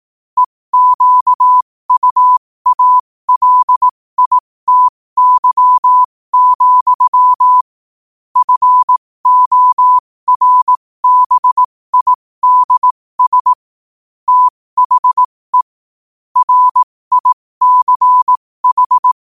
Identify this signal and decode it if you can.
Morse code, heard as a keyed tone, about 18 wpm, spaced as for 15 wpm, 1000 Hz. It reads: EQUALITY, FORBIDS THE RICH
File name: qotdfiles / 15wpm